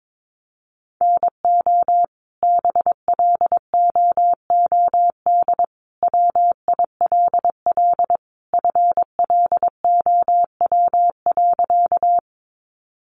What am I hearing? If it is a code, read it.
Morse code, heard as a keyed tone, about 22 wpm, 700 Hz. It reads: NO BLOOD WILL FLOW.
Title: Sentences from Top 1200 Words 22wpm